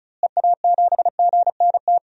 Morse code, heard as a keyed tone, about 36 wpm, 700 Hz. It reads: EA7GNT
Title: DXCC-100 Call Signs - 6 Characters - Repeated 3x - 35wpm